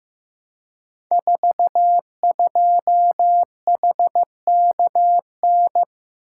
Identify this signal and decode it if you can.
Morse code, heard as a keyed tone, about 15 wpm, 700 Hz. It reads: 42HKN
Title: Code Groups - 5 Letter:Numbers - 15wpm